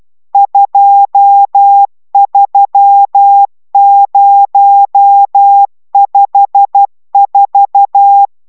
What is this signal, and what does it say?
Morse code, heard as a keyed tone, about 12 wpm, 800 Hz. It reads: 23054
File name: taviro tanulas / cw 69 22 10